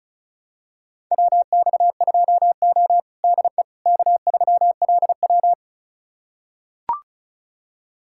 Morse code, heard as a keyed tone, about 34 wpm, 700 Hz. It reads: WX2O DE K3LW
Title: USA Callsign DE Callsign - 4 Characters - 35wpm